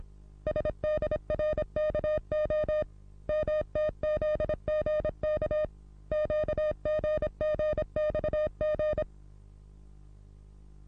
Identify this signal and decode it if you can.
Morse code, heard as a keyed tone, about 26 wpm, 610 Hz. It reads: SDRKO MTZGK QGGXG
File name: repos / Morsekurs / DJ4UF Morsen27